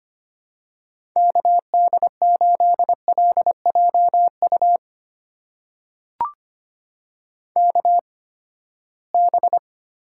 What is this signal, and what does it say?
Morse code, heard as a keyed tone, about 25 wpm, spaced as for 17 wpm, 700 Hz. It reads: KD8LJU KB
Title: USA Call Signs - 8x Character Spacing - 25wpm